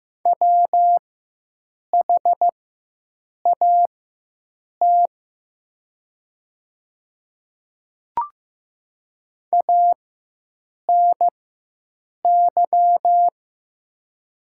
Morse code, heard as a keyed tone, about 15 wpm, spaced as for 7 wpm, 700 Hz. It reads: WHAT ANY
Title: 4x Character Spacing - Top 200 Words - 15wpm